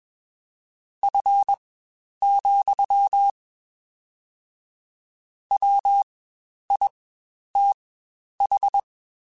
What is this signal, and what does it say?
Morse code, heard as a keyed tone, about 21 wpm, spaced as for 10 wpm, 800 Hz. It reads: F, WITH
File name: Spaced Characters / Spaced Characters-mp3s / Weather 4 21 sp @20